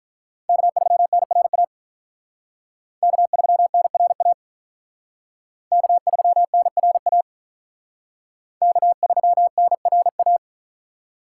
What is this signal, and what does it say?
Morse code, heard as a keyed tone, about 45 wpm, 700 Hz. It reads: K3NRA K3NRA K3NRA K3NRA